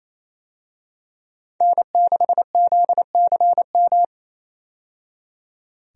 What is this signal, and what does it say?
Morse code, heard as a keyed tone, about 28 wpm, 700 Hz. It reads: N6ZCM
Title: USA Call Signs - 5 Characters - 28wpm